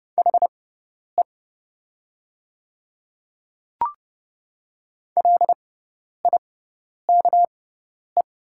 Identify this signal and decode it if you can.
Morse code, heard as a keyed tone, about 31 wpm, spaced as for 10 wpm, 700 Hz. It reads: HE LIKE
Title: 6x Character Spacing - Top 100 Words - 30wpm